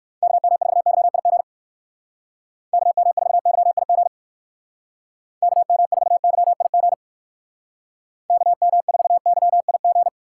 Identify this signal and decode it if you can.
Morse code, heard as a keyed tone, about 59 wpm, 700 Hz. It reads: KM4YIG KM4YIG KM4YIG KM4YIG